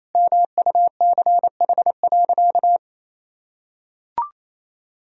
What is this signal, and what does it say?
Morse code, heard as a keyed tone, about 28 wpm, 700 Hz. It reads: MUCH.
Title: Sentences from Top 1000 Words - Review Part 1 - 28wpm